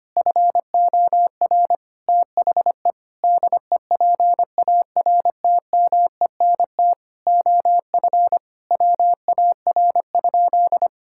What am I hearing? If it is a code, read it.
Morse code, heard as a keyed tone, about 25 wpm, 700 Hz. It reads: FOR THE DEPARTMENT OF WAR?
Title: Sentences from Top 1000 Words - Review Part 5 - 25wpm